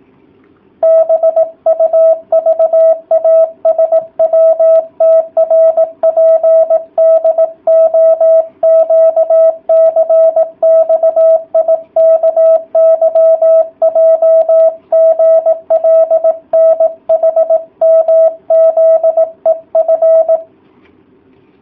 Morse code, heard as a keyed tone, about 18 wpm, 650 Hz. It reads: BUVASWTRPDOQCXIKYJGLNHMZEF